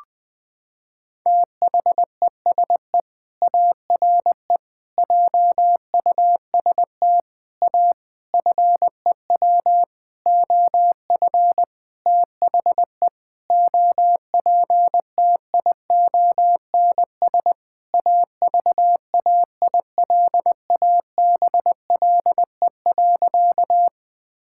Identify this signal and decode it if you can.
Morse code, heard as a keyed tone, about 20 wpm, 700 Hz. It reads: THESE ARE JUST A FEW OF THE OPTIONS AVAILABLE.
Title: Sentences with 7-Word Phrases - 20wpm